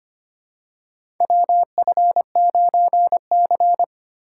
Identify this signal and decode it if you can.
Morse code, heard as a keyed tone, about 25 wpm, 700 Hz. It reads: WF9C
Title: USA Call Signs - 4x Character Spacing - 25wpm